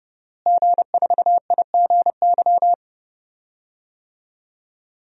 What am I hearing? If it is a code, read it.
Morse code, heard as a keyed tone, about 30 wpm, 700 Hz. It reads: G4IGY
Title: DXCC-100 Call Signs - 5 Characters - Repeated 2x - 30wpm